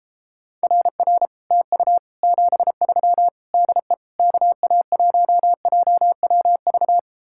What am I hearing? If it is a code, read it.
Morse code, heard as a keyed tone, about 31 wpm, 700 Hz. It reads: RR TU 73 DE KA1JWV